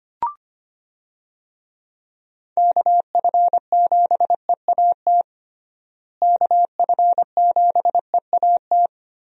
Morse code, heard as a keyed tone, about 25 wpm, 700 Hz. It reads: KF7EAT KF7EAT
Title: USA Call Signs - 6 Characters - Repeated 3x - 25wpm